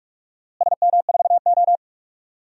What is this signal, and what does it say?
Morse code, heard as a keyed tone, about 45 wpm, 700 Hz. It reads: SM4O
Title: DXCC-100 Call Signs - 4 Characters - Repeated 3x - 45wpm